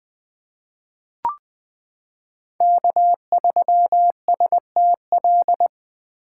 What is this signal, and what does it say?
Morse code, heard as a keyed tone, about 20 wpm, 700 Hz. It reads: K3STL
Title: 2023 POTA Hunter Call Signs - 20wpm